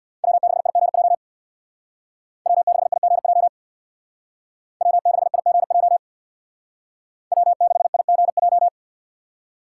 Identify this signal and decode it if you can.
Morse code, heard as a keyed tone, about 57 wpm, 700 Hz. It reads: W6IGJ W6IGJ W6IGJ W6IGJ
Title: Speed Racing - USA 1x3 Call Signs 50wpm